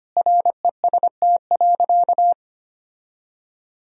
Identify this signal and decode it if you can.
Morse code, heard as a keyed tone, about 25 wpm, 700 Hz. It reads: REST.
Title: Sentences from Top 1000 Words - Review Part 2 - 25wpm